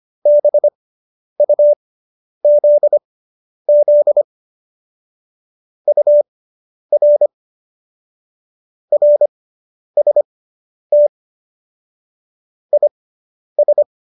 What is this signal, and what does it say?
Morse code, heard as a keyed tone, about 25 wpm, spaced as for 10 wpm, 590 Hz. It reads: BUZZ UR RST IS